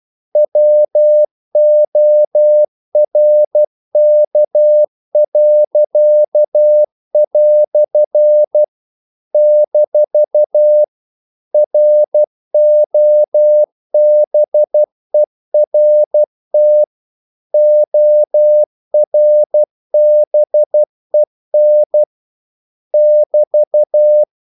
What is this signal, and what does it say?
Morse code, heard as a keyed tone, about 12 wpm, 600 Hz. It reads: WORK." - ROBERT ORBEN <BT>
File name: CW / Practice / EnglishQuotations0312